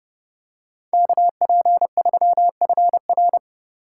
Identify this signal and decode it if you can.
Morse code, heard as a keyed tone, about 31 wpm, 700 Hz. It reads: KP3FR